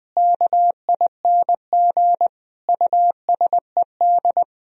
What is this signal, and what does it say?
Morse code, heard as a keyed tone, about 20 wpm, 700 Hz. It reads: KING USED
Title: Sentences from Top 1000 Words - Review Part 4 - 20wpm